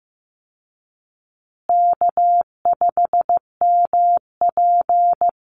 Morse code, heard as a keyed tone, about 15 wpm, 700 Hz. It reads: K5MP
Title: Call Signs Repeated 2x - 15wpm